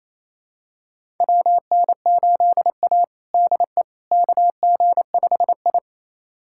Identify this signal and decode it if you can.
Morse code, heard as a keyed tone, about 27 wpm, 700 Hz. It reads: WN8A DE KG5I
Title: USA Callsign DE Callsign - 4 Characters - 28wpm